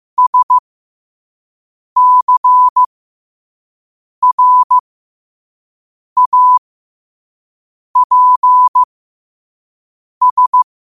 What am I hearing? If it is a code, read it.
Morse code, heard as a keyed tone, about 15 wpm, spaced as for 5 wpm, 1000 Hz. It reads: SCRAPS
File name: qotdfiles / 5wpm